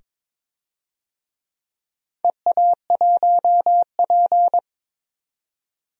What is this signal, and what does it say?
Morse code, heard as a keyed tone, about 22 wpm, 700 Hz. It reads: EA1P
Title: DXCC-100 Call Signs - 4 Characters - 22wpm